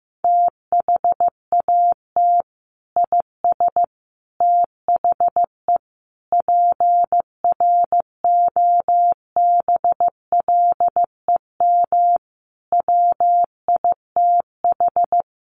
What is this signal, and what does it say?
Morse code, heard as a keyed tone, about 15 wpm, 700 Hz. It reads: THAT IS THE PROBLEM WITH
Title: Sentences from Top 1100 Words 15wpm